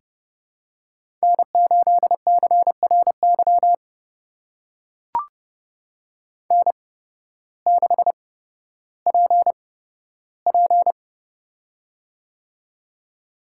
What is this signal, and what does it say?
Morse code, heard as a keyed tone, about 30 wpm, spaced as for 14 wpm, 700 Hz. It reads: N8CRY N6PP
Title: USA Call Signs - 8x Character Spacing - 30wpm